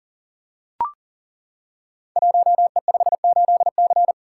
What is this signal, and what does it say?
Morse code, heard as a keyed tone, about 40 wpm, 690 Hz. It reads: 1E58C
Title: Code Groups - 5 Letter:Numbers - 40wpm